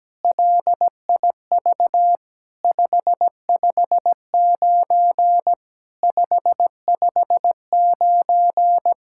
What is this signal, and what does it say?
Morse code, heard as a keyed tone, about 17 wpm, 700 Hz. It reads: LIV 559 559
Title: SOTA QSO Practice 17wpm